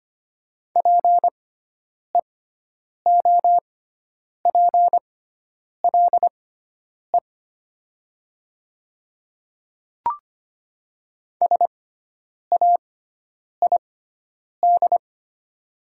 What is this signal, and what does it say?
Morse code, heard as a keyed tone, about 25 wpm, spaced as for 9 wpm, 700 Hz. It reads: PEOPLE SAID
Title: 6x Character Spacing - Top 200 Words - 25wpm